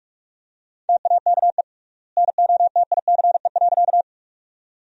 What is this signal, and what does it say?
Morse code, heard as a keyed tone, about 46 wpm, 700 Hz. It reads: TAKE NOTICE.
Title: 3x Word-Spacing - Sentences from Top 400 Words - 45wpm